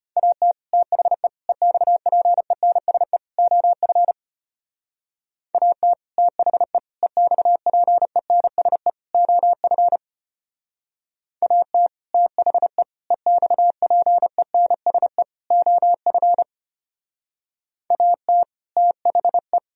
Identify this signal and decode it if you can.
Morse code, heard as a keyed tone, about 32 wpm, 700 Hz. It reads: AT THE EXPENSE OF AT THE EXPENSE OF AT THE EXPENSE OF AT THE